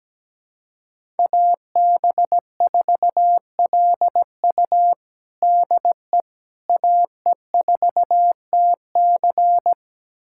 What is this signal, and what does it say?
Morse code, heard as a keyed tone, about 17 wpm, 700 Hz. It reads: AB4LU DE AE4TC
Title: USA Callsign DE Callsign - 5 Characters - 17wpm